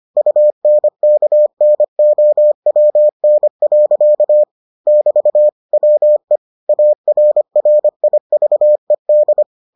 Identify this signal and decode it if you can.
Morse code, heard as a keyed tone, about 25 wpm, 600 Hz. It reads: UNKNOWN. <BT> WE ARRIVED